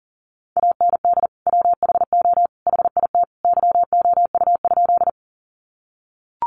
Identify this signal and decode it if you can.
Morse code, heard as a keyed tone, about 40 wpm, 700 Hz. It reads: AND WHO HIT YOU?
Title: Sentences from Top 1200 Words 40wpm